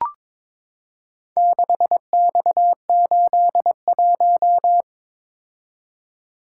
Morse code, heard as a keyed tone, about 22 wpm, 700 Hz. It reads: 6X81
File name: practice / 22wpm / Four Letter-Number Combinations - 22wpm